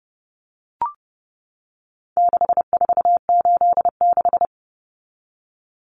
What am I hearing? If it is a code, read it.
Morse code, heard as a keyed tone, about 28 wpm, 700 Hz. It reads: 6486
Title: Numbers - 4-Digit 30wpm